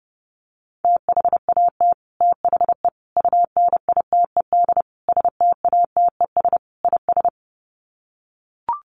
Morse code, heard as a keyed tone, about 30 wpm, 700 Hz. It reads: THAT THE UNITED STATES IS
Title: Sets of 5 Words Encore 30wpm